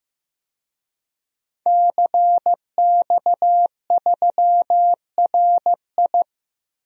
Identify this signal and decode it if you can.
Morse code, heard as a keyed tone, about 15 wpm, 700 Hz. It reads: CX3RI
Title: DXCC-100 Call Signs - 5 Characters - Repeated 3x - 15wpm